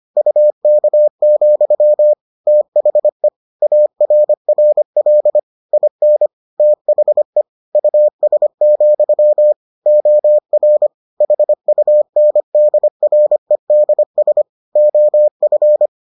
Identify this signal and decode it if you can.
Morse code, heard as a keyed tone, about 25 wpm, 600 Hz. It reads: UK, THE ARRL IN THE US, OR HUNDREDS OF